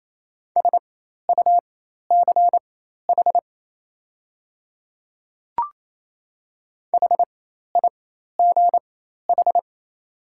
Morse code, heard as a keyed tone, about 27 wpm, spaced as for 13 wpm, 700 Hz. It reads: SUCH HIGH
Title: 4x Character Spacing - Top 200 Words - 28wpm